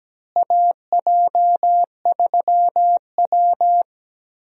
Morse code, heard as a keyed tone, about 17 wpm, 700 Hz. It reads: AJ3W
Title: USA Call Signs - 4 Characters - Repeated 2x - 17wpm